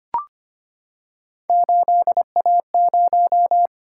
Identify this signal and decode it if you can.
Morse code, heard as a keyed tone, about 25 wpm, 700 Hz. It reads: 8A0
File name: Three Letter-Number Combinations - 25wpm